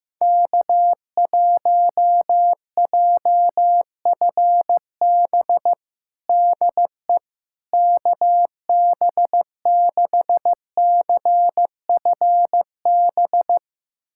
Morse code, heard as a keyed tone, about 16 wpm, 700 Hz. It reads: K1JFB DE KB6CFB
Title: USA Callsign DE Callsign 15wpm